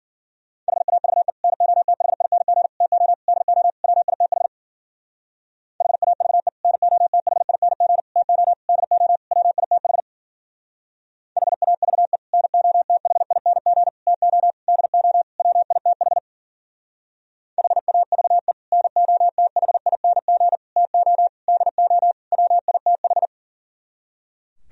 Morse code, heard as a keyed tone, about 52 wpm, 700 Hz. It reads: HAVE NOTHING TO DO WITH HAVE NOTHING TO DO WITH HAVE NOTHING TO DO WITH HAVE NOTHING TO DO WITH